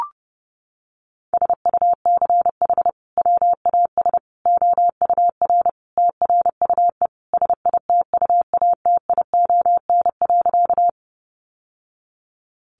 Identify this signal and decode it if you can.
Morse code, heard as a keyed tone, about 30 wpm, 700 Hz. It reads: SUCH WAS OUR TRUE SITUATION.
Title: Sentences from Top 1200 Words 30wpm